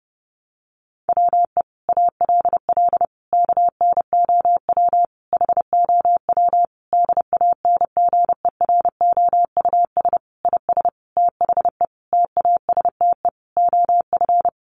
Morse code, heard as a keyed tone, about 30 wpm, 700 Hz. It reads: WE ALL KNOW HOW DANGEROUS IS THE TASTE OF
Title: Sentences from Top 1200 Words 30wpm